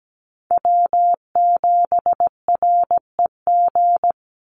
Morse code, heard as a keyed tone, about 17 wpm, 700 Hz. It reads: W7REG
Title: USA Call Signs 17wpm